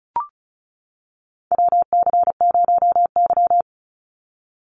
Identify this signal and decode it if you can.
Morse code, heard as a keyed tone, about 34 wpm, 700 Hz. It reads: WC0Y
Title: NA SOTA Chaser Callsigns 35wpm